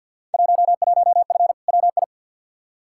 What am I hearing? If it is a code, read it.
Morse code, heard as a keyed tone, about 52 wpm, 700 Hz. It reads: 11F WI
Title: Field Day Exchanges Encore - 50wpm